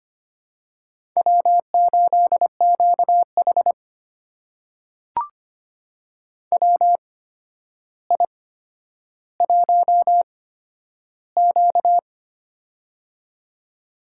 Morse code, heard as a keyed tone, about 25 wpm, spaced as for 11 wpm, 700 Hz. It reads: W8QH WI1Q
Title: USA Call Signs - 4 Characters - 8x Character Spacing - 25wpm